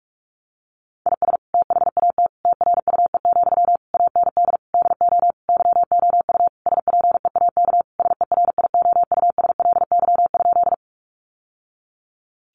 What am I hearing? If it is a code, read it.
Morse code, heard as a keyed tone, about 44 wpm, 700 Hz. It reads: IS THAT TRUE, AND DO YOU SPEAK SERIOUSLY?
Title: Sentences from Top 1200 Words 45wpm